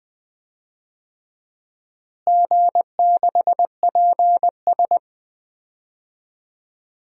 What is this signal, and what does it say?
Morse code, heard as a keyed tone, about 20 wpm, 700 Hz. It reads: G6PS
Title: DXCC-100 Call Signs - 4 Characters - Repeated 3x - 20wpm